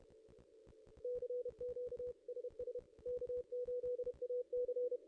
Morse code, heard as a keyed tone, about 31 wpm, 490 Hz. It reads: CQ SS K8AC
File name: contests / ss-cw-k5tr-op-n5rz -1600